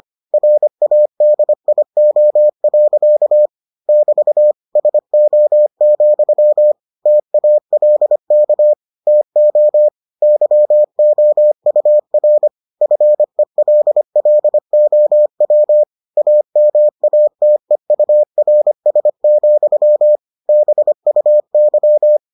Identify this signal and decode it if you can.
Morse code, heard as a keyed tone, about 25 wpm, 600 Hz. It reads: RADIO. <BT> SO, TALK TO YOUR FELLOW AMATEURS, BUY